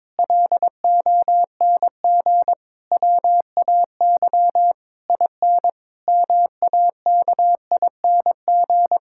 Morse code, heard as a keyed tone, about 21 wpm, 700 Hz. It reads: LONG WAY IN MAKING
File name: Difficult Sentences with Hints - 22wpm